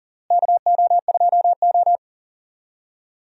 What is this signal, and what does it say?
Morse code, heard as a keyed tone, about 37 wpm, 700 Hz. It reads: KO2O
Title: USA Call Signs - 4 Characters - Repeated 3x - 40wpm